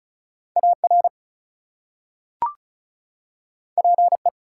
Morse code, heard as a keyed tone, about 36 wpm, 700 Hz. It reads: AR PE
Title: US States and Canadian Provinces 35wpm